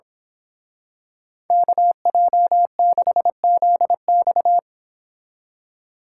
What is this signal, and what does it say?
Morse code, heard as a keyed tone, about 26 wpm, 700 Hz. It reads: KJ6ZX
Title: Speed Racing - USA 2x2 Call Signs 22wpm